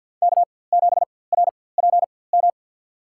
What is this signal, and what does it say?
Morse code, heard as a keyed tone, about 53 wpm, spaced as for 24 wpm, 700 Hz. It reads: K7RPM
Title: USA Call Signs - 4x Character Spacing - 50wpm